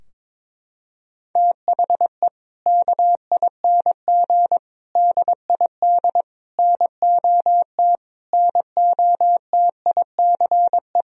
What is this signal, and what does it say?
Morse code, heard as a keyed tone, about 22 wpm, 700 Hz. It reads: THE KING DID NOT NOTICE
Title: Sentences from Top 1000 Words - Review Part 2 - 22wpm